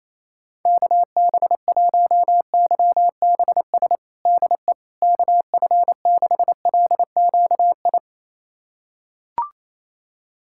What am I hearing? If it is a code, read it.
Morse code, heard as a keyed tone, about 27 wpm, 700 Hz. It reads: KB1YBS DE KF6LQI
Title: USA Callsign DE Callsign - 6 Characters - 28wpm